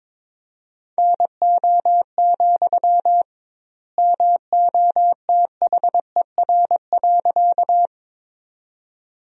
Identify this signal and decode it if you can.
Morse code, heard as a keyed tone, about 21 wpm, 700 Hz. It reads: NO, MOTHER.
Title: 2x Word-Spacing - Sentences from Top 200 Words - 22wpm